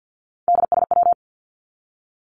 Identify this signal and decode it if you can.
Morse code, heard as a keyed tone, about 50 wpm, 700 Hz. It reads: DSW
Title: Abbreviations 50wpm